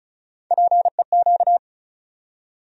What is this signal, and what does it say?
Morse code, heard as a keyed tone, about 36 wpm, 700 Hz. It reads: PEQ
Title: USA Call Sign Postfixes - 3 Letters 35wpm